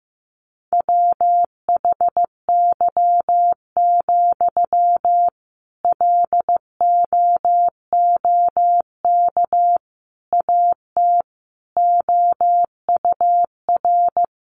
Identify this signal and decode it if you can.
Morse code, heard as a keyed tone, about 15 wpm, 700 Hz. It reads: WHY, LOOK AT OUR
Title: Sentences from Top 600 Words 15wpm